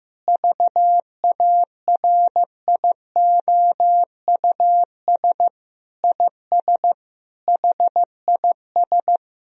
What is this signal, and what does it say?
Morse code, heard as a keyed tone, about 15 wpm, 700 Hz. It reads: VARIOUS IS HIS
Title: Sentences from Top 1000 Words - Review Part 4 - 15wpm